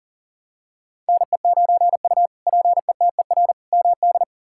Morse code, heard as a keyed tone, about 40 wpm, 700 Hz. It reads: NE9U PETER MD
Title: K1USN SST QSO Practice - 40wpm